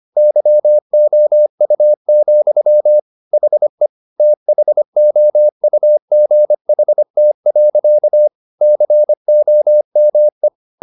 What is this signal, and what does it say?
Morse code, heard as a keyed tone, about 25 wpm, 600 Hz. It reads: YOU, HE THOUGHT. COME
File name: the old man and the sea part 1 25wpm